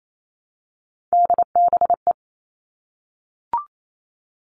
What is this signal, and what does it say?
Morse code, heard as a keyed tone, about 28 wpm, 700 Hz. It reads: DBE
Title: All Cut Numbers - 3-Digit 28wpm